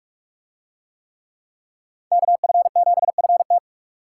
Morse code, heard as a keyed tone, about 44 wpm, 700 Hz. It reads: KF7FT